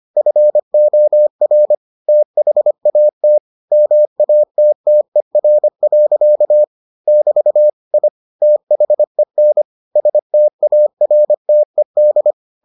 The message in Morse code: FOR THAT MATTER. <BT> I THEN STARTED

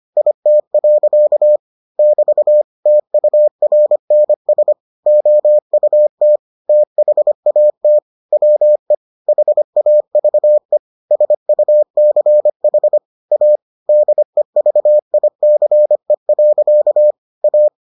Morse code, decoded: IT. <BT> TURNS OUT THAT WE HAVE SUCH A DEVICE. A

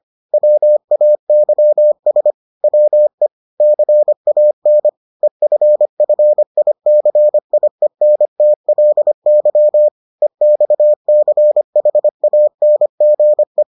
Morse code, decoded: WAYS WE CAN EFFICIENTLY EXCHANGE

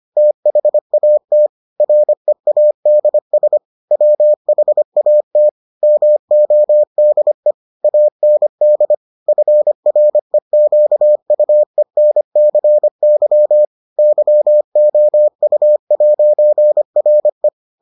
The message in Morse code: THAT READS WHAT MODE AND FREQUENCY YOU'RE

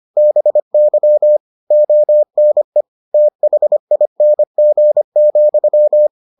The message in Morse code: BY ONE THING,

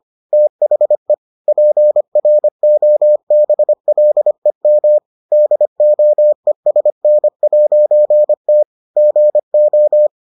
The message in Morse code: THE PROBLEM DOESN'T GO

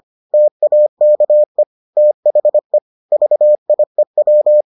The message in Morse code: TAKE THE VIEW

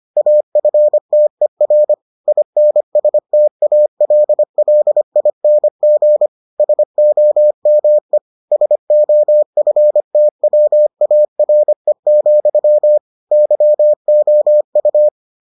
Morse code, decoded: AFTER INSTALLING SOME SOFTWARE, YOU

